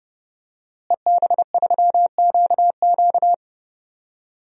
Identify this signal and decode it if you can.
Morse code, heard as a keyed tone, about 30 wpm, 700 Hz. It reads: EB3QQ
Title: DXCC-100 Call Signs - 5 Characters - Repeated 2x - 30wpm